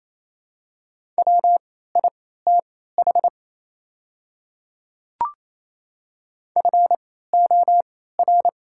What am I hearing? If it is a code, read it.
Morse code, heard as a keyed tone, about 28 wpm, spaced as for 16 wpm, 690 Hz. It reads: WITH FOR
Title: 3x Character Spacing - Top 100 Words - 28wpm